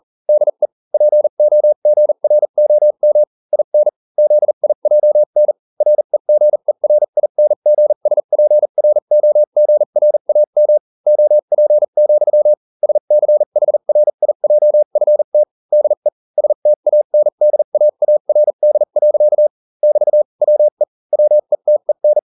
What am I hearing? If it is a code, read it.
Morse code, heard as a keyed tone, about 40 wpm, spaced as for 35 wpm, 600 Hz. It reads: DE POGROM IN ZIJN REGERINGSPROGRAM OP, SCHRIJFT DE STANDAARD. <BT> WE WETEN